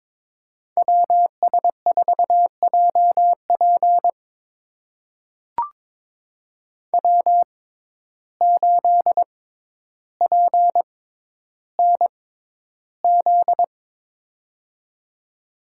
Morse code, heard as a keyed tone, about 22 wpm, spaced as for 11 wpm, 700 Hz. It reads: WS4JP W8PNZ
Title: USA Call Signs - 5 Characters - 6x Character Spacing - 22wpm